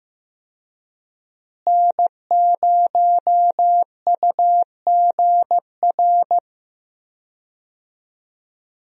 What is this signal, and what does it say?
Morse code, heard as a keyed tone, about 15 wpm, 700 Hz. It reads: N0UGR
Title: USA Call Signs - 5 Characters - Repeated 3x - 15wpm